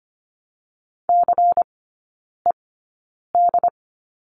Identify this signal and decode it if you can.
Morse code, heard as a keyed tone, about 25 wpm, spaced as for 9 wpm, 700 Hz. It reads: CED
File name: course / 25:10wpm / 093 Lesson 16e - Letter c - Three Characters 25wpm